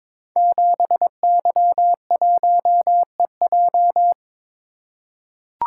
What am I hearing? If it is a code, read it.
Morse code, heard as a keyed tone, about 22 wpm, 700 Hz. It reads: 7Y1EJ
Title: Code Groups - 5 Letter:Numbers - 22wpm